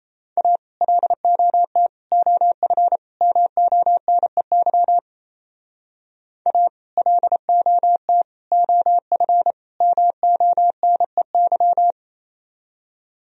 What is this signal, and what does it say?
Morse code, heard as a keyed tone, about 32 wpm, 700 Hz. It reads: A LOT OF MONEY A LOT OF MONEY
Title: Speed-Racing - Sets of 4 Words 28wpm